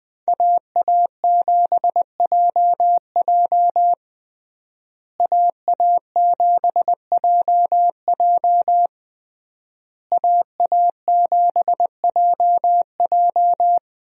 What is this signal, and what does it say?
Morse code, heard as a keyed tone, about 20 wpm, 700 Hz. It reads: AA7JJ AA7JJ AA7JJ
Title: USA Call Signs - 5 Characters - Repeated 3x - 20wpm